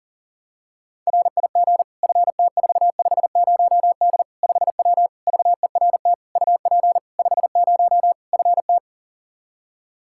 Morse code, heard as a keyed tone, about 40 wpm, 700 Hz. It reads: RIG FT450D 5W VERT UP 50 FT
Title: Practice QSO - Exchange 2 - Info - 40wpm